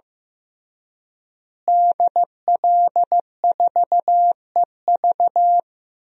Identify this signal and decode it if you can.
Morse code, heard as a keyed tone, about 15 wpm, 700 Hz. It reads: DL4EV
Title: DXCC-100 Call Signs - 5 Characters - Repeated 3x - 15wpm